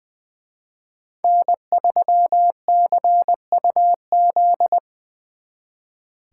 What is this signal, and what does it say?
Morse code, heard as a keyed tone, about 24 wpm, 700 Hz. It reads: N3CUZ
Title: Speed Racing - USA Call Signs 20wpm